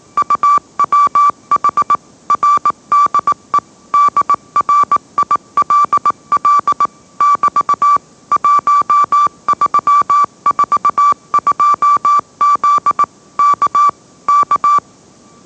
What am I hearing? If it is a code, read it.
Morse code, heard as a keyed tone, about 18 wpm, 1160 Hz. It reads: UWH RDE DRILL <BT> 1342Z K K
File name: mrx / mrx082 20wpm